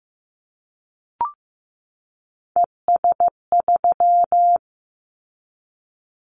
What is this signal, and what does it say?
Morse code, heard as a keyed tone, about 15 wpm, 700 Hz. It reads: ES3
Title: Callsign Prefixes for Top 50 DXCC - 15wpm